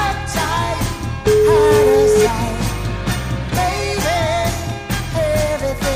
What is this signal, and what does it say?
Ringback tone (Europe, 425 Hz), with noise.